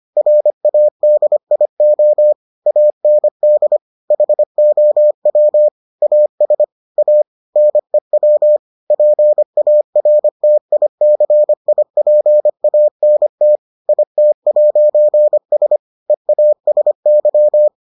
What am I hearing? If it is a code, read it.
Morse code, heard as a keyed tone, about 25 wpm, 600 Hz. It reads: RADIO AND HOW AS A NEW PARTICIPANT IT'S EASY